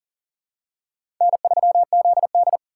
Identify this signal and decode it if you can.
Morse code, heard as a keyed tone, about 40 wpm, 700 Hz. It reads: N3ZD
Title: USA Call Signs - 4 Characters - 6x Character Spacing - 40wpm